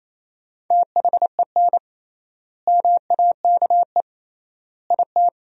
Morse code, heard as a keyed tone, about 28 wpm, 700 Hz. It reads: THEN MAKE IT
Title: 3x Word-Spacing - Sentences from Top 100 Words - 28wpm